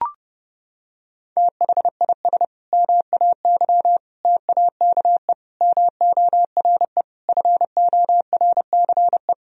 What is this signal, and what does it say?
Morse code, heard as a keyed tone, about 30 wpm, 700 Hz. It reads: THIS MAY TAKE MORE FORCE
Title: Sentences from Top 1000 Words - Review Part 3 - 30wpm